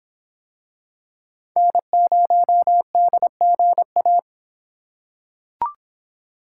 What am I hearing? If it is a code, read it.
Morse code, heard as a keyed tone, about 26 wpm, 700 Hz. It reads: N0DGA